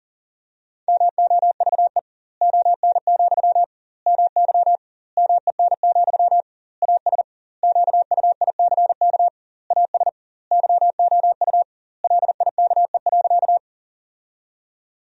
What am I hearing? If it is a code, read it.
Morse code, heard as a keyed tone, about 37 wpm, 700 Hz. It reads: MOVE ON, MY MEN, AS QUICK AS YOU LIKE.
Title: 2x Word-Spacing - Sentences from Top 500 Words - 40wpm